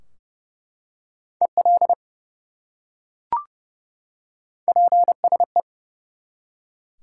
Morse code, heard as a keyed tone, about 29 wpm, 700 Hz. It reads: EL PSE
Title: Top 180 QSO Elements 30wpm